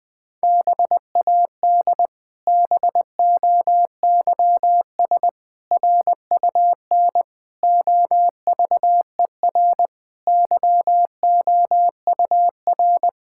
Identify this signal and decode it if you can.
Morse code, heard as a keyed tone, about 21 wpm, 700 Hz. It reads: BAD BOYS RUN OVER YOUR
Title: Pangram Challenge v2 - 20wpm